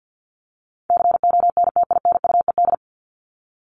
Morse code, heard as a keyed tone, about 50 wpm, 700 Hz. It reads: CONTINUED
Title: ED Suffix - Top 100 Words with Base Words 50wpm